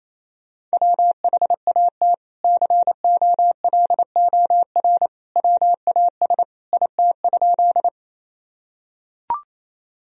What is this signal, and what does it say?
Morse code, heard as a keyed tone, about 28 wpm, 700 Hz. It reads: WHAT COLOR WAS IT?
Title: Sentences from Top 1000 Words - Review Part 2 - 28wpm